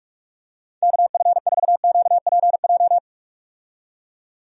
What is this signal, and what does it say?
Morse code, heard as a keyed tone, about 44 wpm, 700 Hz. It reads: KF4QPJ
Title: USA Call Signs - 6 Characters - 45wpm